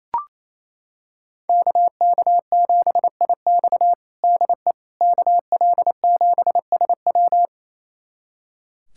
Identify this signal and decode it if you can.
Morse code, heard as a keyed tone, about 28 wpm, 700 Hz. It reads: KK7IX DE KL7SW
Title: USA Callsign DE Callsign - 5 Characters - 28wpm